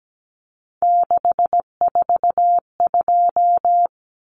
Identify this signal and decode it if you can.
Morse code, heard as a keyed tone, about 17 wpm, 700 Hz. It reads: 642
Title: Numbers - 3-Digit 17wpm